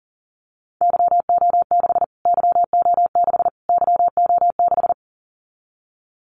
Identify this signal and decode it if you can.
Morse code, heard as a keyed tone, about 40 wpm, 700 Hz. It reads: YO6 YO6 YO6
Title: Callsign Prefixes for Top 25 DXCC - Repeated 3x - 40wpm